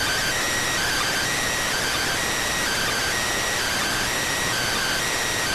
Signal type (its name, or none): Emergency Alert System (SAME) tones